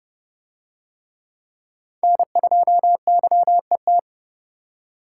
Morse code, heard as a keyed tone, about 30 wpm, 700 Hz. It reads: N2YET